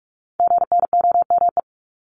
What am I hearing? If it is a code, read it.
Morse code, heard as a keyed tone, about 46 wpm, 700 Hz. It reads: GNOME
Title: Multi-letter Phonemes Repeated 3x 45wpm